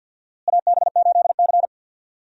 Morse code, heard as a keyed tone, about 51 wpm, 700 Hz. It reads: AB8C
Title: USA Call Signs - 4 Characters - Repeated 2x - 50wpm